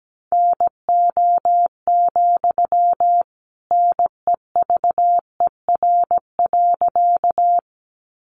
Morse code, heard as a keyed tone, about 17 wpm, 700 Hz. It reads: NO, NEVER.
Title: Sentences from Top 300 Words 17wpm